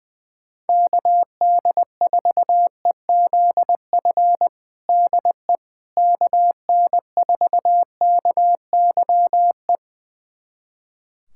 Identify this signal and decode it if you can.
Morse code, heard as a keyed tone, about 21 wpm, 700 Hz. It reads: KD4EZF DE KN4KYE
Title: USA Callsign DE Callsign 20wpm